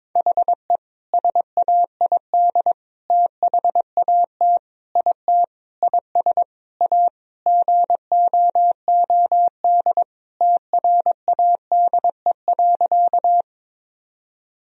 Morse code, heard as a keyed tone, about 22 wpm, 700 Hz. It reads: HE SAID THAT IT IS A GOOD TRADE.
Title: Sentences from Top 1000 Words - Review Part 4 - 22wpm